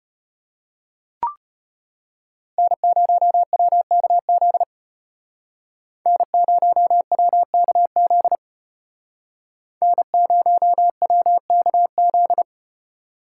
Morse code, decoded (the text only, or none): N0WKZ N0WKZ N0WKZ